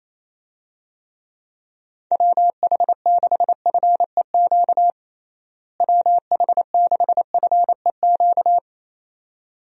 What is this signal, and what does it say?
Morse code, heard as a keyed tone, about 27 wpm, 700 Hz. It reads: WH6FEQ WH6FEQ
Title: USA Call Signs - 6 Characters - Repeated 2x - 28wpm